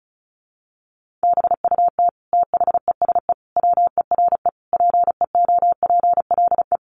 Morse code, heard as a keyed tone, about 34 wpm, 700 Hz. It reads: BUT THESE WERE PEOPLE
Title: Sentences from Top 1500 Words 35wpm